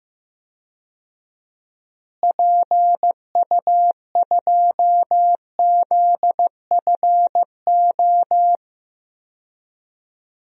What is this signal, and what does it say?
Morse code, heard as a keyed tone, about 15 wpm, 700 Hz. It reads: PU2ZFO
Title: DXCC-100 Call Signs - 6 Characters - Repeated 2x - 15wpm